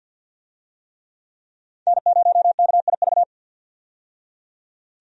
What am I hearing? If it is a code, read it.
Morse code, heard as a keyed tone, about 51 wpm, 700 Hz. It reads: N0KIV
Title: USA Call Signs - 5 Characters - 50wpm